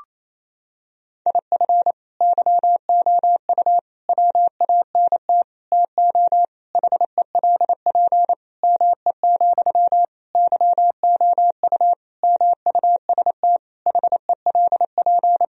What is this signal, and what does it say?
Morse code, heard as a keyed tone, about 28 wpm, 700 Hz. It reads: IF YOU WANT TO HELP ME, YOU MUST HELP
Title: Sentences from Top 1000 Words - Review Part 1 - 28wpm